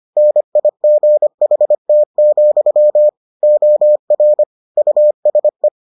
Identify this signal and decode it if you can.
Morse code, heard as a keyed tone, about 25 wpm, 600 Hz. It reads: NIGHT, OR USE